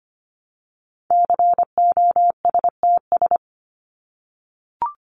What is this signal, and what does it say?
Morse code, heard as a keyed tone, about 25 wpm, 700 Hz. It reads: COSTS
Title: ES:S Suffix - Top 100 Words with Base Words 25wpm